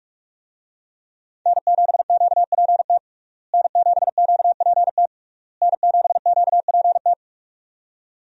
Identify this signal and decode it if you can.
Morse code, heard as a keyed tone, about 46 wpm, 700 Hz. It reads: N7QPT N7QPT N7QPT
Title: USA Call Signs - 5 Characters - Repeated 3x - 45wpm